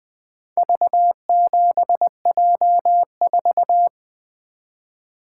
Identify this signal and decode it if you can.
Morse code, heard as a keyed tone, about 25 wpm, 700 Hz. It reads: V7J4
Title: Speed-Racing - Four Letter-Number Combinations - 20wpm